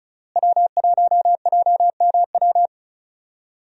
Morse code, heard as a keyed tone, about 34 wpm, 700 Hz. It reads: W1JMW
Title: USA Call Signs - 5 Characters - Repeated 3x - 35wpm